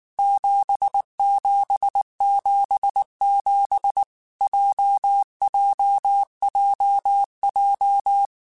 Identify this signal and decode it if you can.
Morse code, heard as a keyed tone, about 19 wpm, 800 Hz. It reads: 7777 JJJJ
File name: Basic Sets / Basic Letter Groups Collective7 / JZ17@18